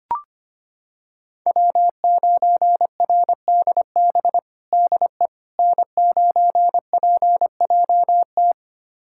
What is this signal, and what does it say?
Morse code, heard as a keyed tone, about 25 wpm, 700 Hz. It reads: W9RDB DE N9PJT
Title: USA Callsign DE Callsign - 5 Characters - 25wpm